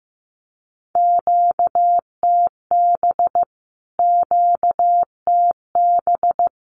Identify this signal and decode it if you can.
Morse code, heard as a keyed tone, about 15 wpm, 700 Hz. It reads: QTB QTB
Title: Q-Codes Repeated 3x 15wpm